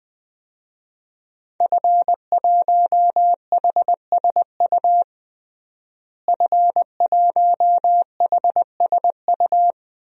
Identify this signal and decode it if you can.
Morse code, heard as a keyed tone, about 20 wpm, 700 Hz. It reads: F1HSU F1HSU
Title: DXCC-100 Call Signs - 5 Characters - Repeated 2x - 20wpm